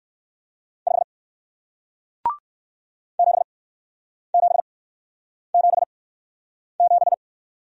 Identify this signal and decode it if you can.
Morse code, heard as a keyed tone, about 55 wpm, spaced as for 9 wpm, 700 Hz. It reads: 5 7777
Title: Speed-Racing Single Letter-Number 45wpm